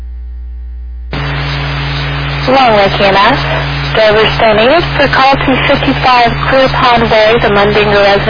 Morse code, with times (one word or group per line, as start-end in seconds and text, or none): none